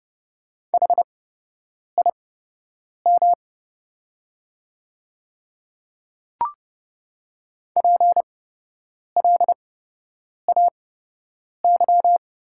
Morse code, heard as a keyed tone, about 30 wpm, spaced as for 8 wpm, 700 Hz. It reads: HIM PLAY
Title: 8x Character Spacing - Top 200 Words - 30wpm